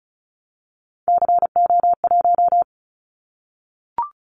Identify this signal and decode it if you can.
Morse code, heard as a keyed tone, about 36 wpm, 700 Hz. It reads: CO1
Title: Callsign Prefixes for Top 100 DXCC - 35wpm